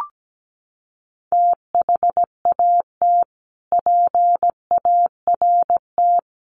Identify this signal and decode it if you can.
Morse code, heard as a keyed tone, about 17 wpm, 700 Hz. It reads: THAT PART
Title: Sets of 5 Words Encore2 17wpm